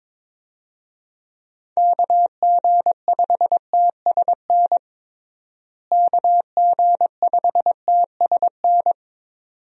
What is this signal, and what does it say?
Morse code, heard as a keyed tone, about 23 wpm, 700 Hz. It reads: KG5TSN KG5TSN
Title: USA Call Signs - 6 Characters - Repeated 2x - 22wpm